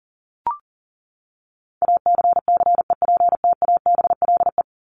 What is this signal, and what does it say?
Morse code, heard as a keyed tone, about 40 wpm, 700 Hz. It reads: ACCEPTABLE
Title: Top 1501-1750 Words 40wpm